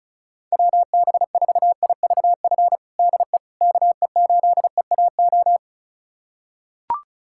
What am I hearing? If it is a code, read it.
Morse code, heard as a keyed tone, about 36 wpm, 700 Hz. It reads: WB4IVF DE KE8EAO
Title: USA Callsign DE Callsign 35wpm